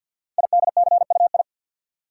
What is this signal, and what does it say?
Morse code, heard as a keyed tone, about 53 wpm, 700 Hz. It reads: IDCUI
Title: Code Groups - 5 Letter:Numbers - 50wpm